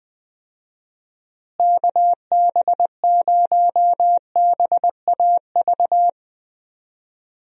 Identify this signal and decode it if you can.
Morse code, heard as a keyed tone, about 20 wpm, 700 Hz. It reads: KB0BAV
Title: USA Call Signs - 6 Characters - Repeated 2x - 20wpm